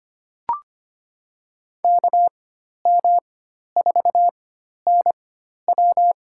Morse code, heard as a keyed tone, about 25 wpm, spaced as for 12 wpm, 700 Hz. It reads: KM4NW
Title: USA Call Signs - 4x Character Spacing - 25wpm